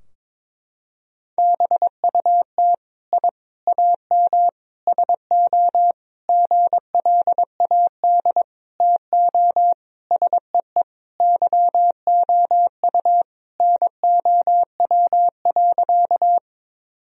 Morse code, heard as a keyed tone, about 22 wpm, 700 Hz. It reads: BUT I AM SO GLAD TO SEE YOU NOW.
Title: Sentences from Top 1000 Words - Review Part 2 - 22wpm